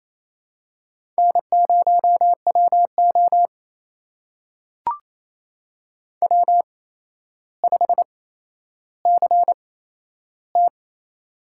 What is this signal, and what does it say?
Morse code, heard as a keyed tone, about 28 wpm, spaced as for 12 wpm, 700 Hz. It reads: N0WO W5CT
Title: USA Call Signs - 4 Characters - 8x Character Spacing - 28wpm